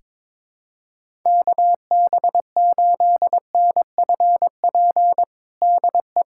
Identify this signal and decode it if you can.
Morse code, heard as a keyed tone, about 21 wpm, 700 Hz. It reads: KB8NFP DE
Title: USA Callsign DE Callsign - 6 Characters - 22wpm